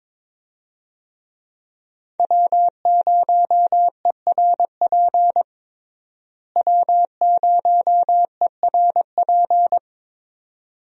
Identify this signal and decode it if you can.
Morse code, heard as a keyed tone, about 22 wpm, 700 Hz. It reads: W0ERP W0ERP
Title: USA Call Signs - 5 Characters - Repeated 2x - 22wpm